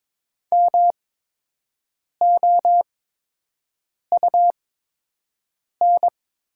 Morse code, heard as a keyed tone, about 23 wpm, spaced as for 6 wpm, 700 Hz. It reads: MOUN
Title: 8x Character Spacing - Top 500 Words - 22wpm